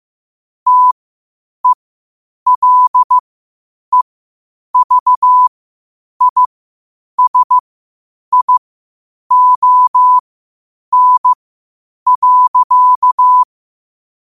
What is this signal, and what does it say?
Morse code, heard as a keyed tone, about 15 wpm, spaced as for 8 wpm, 1000 Hz. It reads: TELEVISION.
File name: 8wpm